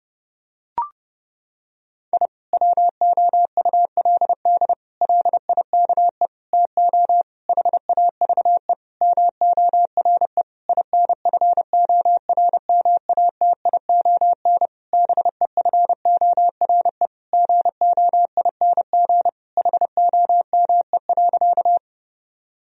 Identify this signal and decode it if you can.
Morse code, heard as a keyed tone, about 32 wpm, 700 Hz. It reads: I WOULD LIKE TO HAVE MORE INFORMATION BEFORE GOING HOME.
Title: Sentences from Top 1000 Words - Review Part 3 - 30wpm